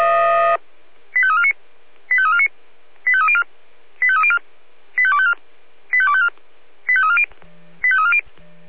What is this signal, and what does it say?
Five-tone selective call (ZVEI), heard as quick runs of five tones, tones of about 70 ms. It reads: 85349 85349 85304 85304 85254 85254 85340 85340
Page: /496/SDR/ZVEI/